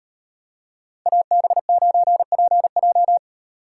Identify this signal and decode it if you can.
Morse code, heard as a keyed tone, about 30 wpm, 700 Hz. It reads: AB9PJ
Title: Speed Racing - USA 2x2 Call Signs 25wpm